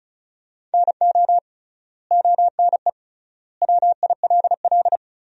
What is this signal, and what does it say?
Morse code, heard as a keyed tone, about 36 wpm, 700 Hz. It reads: NO ONE WILL
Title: 3x Word-Spacing - Sentences from Top 100 Words - 35wpm